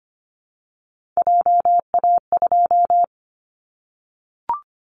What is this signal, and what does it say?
Morse code, heard as a keyed tone, about 25 wpm, 700 Hz. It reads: JA2
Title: Callsign Prefixes for Top 25 DXCC - 25wpm